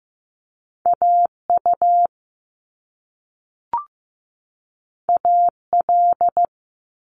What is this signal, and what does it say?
Morse code, heard as a keyed tone, about 15 wpm, 700 Hz. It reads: AU AL
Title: Top Two Letter Combinations in Words 15wpm